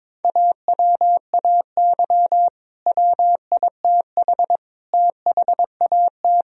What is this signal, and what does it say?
Morse code, heard as a keyed tone, about 22 wpm, 700 Hz. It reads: AWAY WITH THAT
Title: Sentences from Top 1000 Words - Review Part 4 - 22wpm